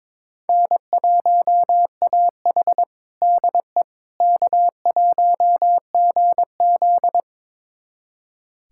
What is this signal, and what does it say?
Morse code, heard as a keyed tone, about 22 wpm, 700 Hz. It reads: N1AH DE K1GZ
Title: USA Callsign DE Callsign - 4 Characters - 22wpm